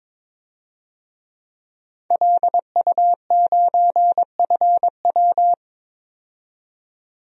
Morse code, heard as a keyed tone, about 21 wpm, 700 Hz. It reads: LU9FW
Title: DXCC-100 Call Signs - 5 Characters - 22wpm